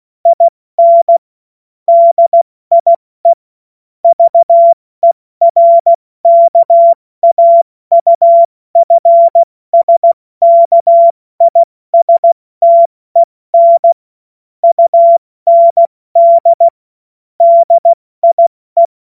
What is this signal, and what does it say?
Morse code, heard as a keyed tone, about 16 wpm, 680 Hz. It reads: IN DIE VERKAUFSKISTEN UND DIE
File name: Witz-16-September-2025-70-F80